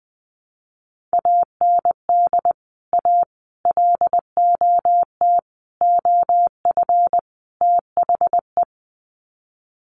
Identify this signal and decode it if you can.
Morse code, heard as a keyed tone, about 20 wpm, 700 Hz. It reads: AND A LOT OF THE
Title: Sets of 5 Words Encore 20wpm